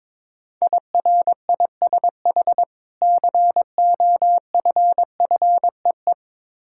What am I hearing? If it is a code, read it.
Morse code, heard as a keyed tone, about 22 wpm, 690 Hz. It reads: IRISH COFFEE
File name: Difficult Words with Hints - 22wpm